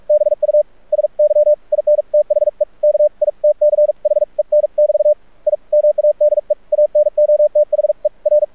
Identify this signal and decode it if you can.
Morse code, heard as a keyed tone, about 44 wpm, spaced as for 37 wpm, 600 Hz. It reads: BUSY IN THE KITCHEN <BT> I MADE ANOTHER
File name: archived / CRSnet